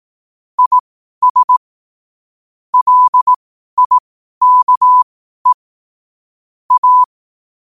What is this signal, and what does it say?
Morse code, heard as a keyed tone, about 18 wpm, spaced as for 12 wpm, 1000 Hz. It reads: IS LIKE A
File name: qotdfiles / 12wpm